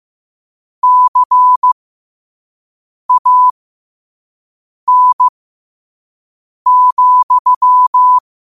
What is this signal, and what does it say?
Morse code, heard as a keyed tone, about 15 wpm, spaced as for 5 wpm, 1000 Hz. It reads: CAN,
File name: qotdfiles / 5wpm